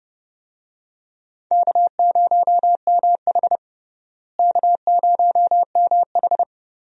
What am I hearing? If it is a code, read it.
Morse code, heard as a keyed tone, about 30 wpm, 700 Hz. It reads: K0MH K0MH
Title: USA Call Signs - 4 Characters - Repeated 3x - 30wpm